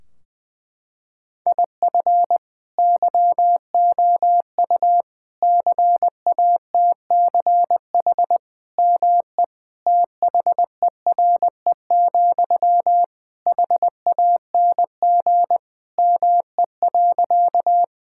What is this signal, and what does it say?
Morse code, heard as a keyed tone, about 21 wpm, 700 Hz. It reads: IF YOU CATCH ME THERE, HANG ME.
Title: Sentences from Top 1600 Words 20wpm